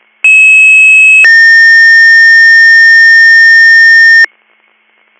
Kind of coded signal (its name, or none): two-tone page